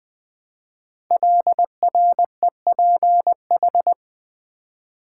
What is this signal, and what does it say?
Morse code, heard as a keyed tone, about 20 wpm, 700 Hz. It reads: LREPH
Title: Code Groups - 5 Letters - 20wpm